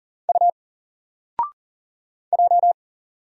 Morse code, heard as a keyed tone, about 40 wpm, 700 Hz. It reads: U J
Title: Single Character (Letter:Number:Punctuation:Prosign) 40wpm